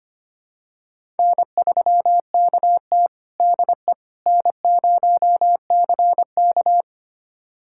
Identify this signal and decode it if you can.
Morse code, heard as a keyed tone, about 25 wpm, 700 Hz. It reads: N3KT DE N0CK
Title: USA Callsign DE Callsign - 4 Characters - 25wpm